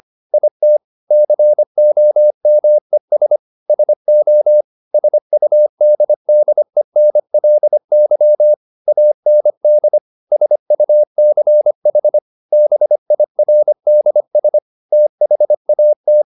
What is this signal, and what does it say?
Morse code, heard as a keyed tone, about 25 wpm, 600 Hz. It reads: IT COMES SO SUDDENLY AND SUCH BIRDS THAT
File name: the old man and the sea part 1 25wpm